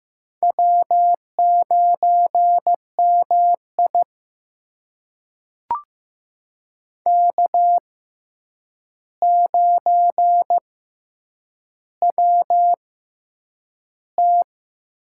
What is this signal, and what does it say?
Morse code, heard as a keyed tone, about 15 wpm, spaced as for 8 wpm, 700 Hz. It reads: W9MI K9WT
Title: USA Call Signs - 4 Characters - 6x Character Spacing - 15wpm